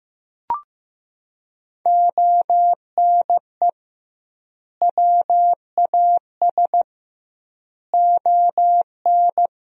Morse code, heard as a keyed tone, about 15 wpm, 700 Hz. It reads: ONE WAS ON
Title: 2x Word-Spacing - Sentences from Top 300 Words - 15wpm